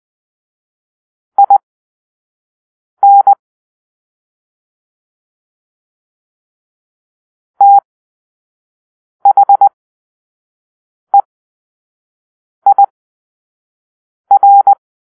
Morse code, heard as a keyed tone, about 20 wpm, spaced as for 5 wpm, 800 Hz. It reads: IN THEIR
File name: arrl / Grp-4 Drill-4